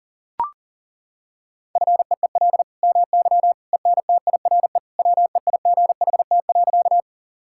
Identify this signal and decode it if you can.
Morse code, heard as a keyed tone, about 40 wpm, 700 Hz. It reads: FEEL MY ENTIRE WEIGHT.
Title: Sentences from Top 1000 Words - Review Part 5 - 40wpm